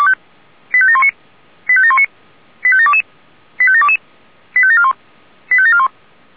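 Five-tone selective call (ZVEI), heard as quick runs of five tones, tones of about 70 ms. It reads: 85719 85719 85722 85722 85741 85741
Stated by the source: ZVEI five-tone selective call received via SDR.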